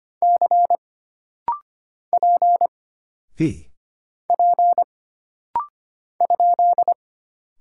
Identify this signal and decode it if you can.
Morse code, heard as a keyed tone, about 25 wpm, 700 Hz. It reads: C P P ?
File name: Single Character (Letter:Number:Punctuation:Prosign) - Rapid-Fire - 25wpm